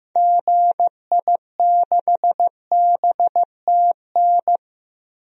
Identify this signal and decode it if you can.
Morse code, heard as a keyed tone, about 15 wpm, 700 Hz. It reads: GI6BTN
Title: DXCC-100 Call Signs - 6 Characters - Repeated 3x - 15wpm